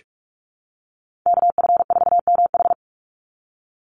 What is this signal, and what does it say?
Morse code, heard as a keyed tone, about 44 wpm, 700 Hz. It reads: KF4MH
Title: NA SOTA Chaser Callsigns 45wpm